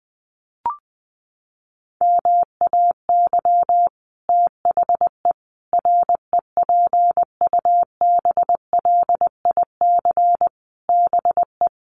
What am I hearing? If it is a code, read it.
Morse code, heard as a keyed tone, about 20 wpm, 700 Hz. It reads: MAY THE REPUBLIC BE
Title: Sentences from Top 1500 Words 20wpm